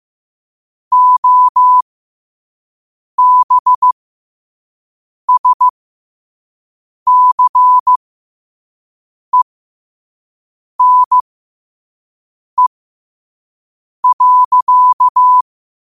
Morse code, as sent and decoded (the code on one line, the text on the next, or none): --- -... ... -.-. -. .-.-.-
OBSCENE.